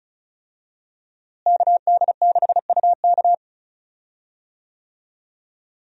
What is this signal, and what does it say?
Morse code, heard as a keyed tone, about 35 wpm, 700 Hz. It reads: KD6UK
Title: USA Call Signs - 5 Characters - Repeated 2x - 35wpm